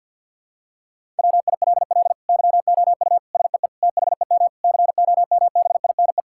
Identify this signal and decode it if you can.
Morse code, heard as a keyed tone, about 50 wpm, 700 Hz. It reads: WILL YOU SEE THEM COMBINE
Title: Sentences from Top 1600 Words 50wpm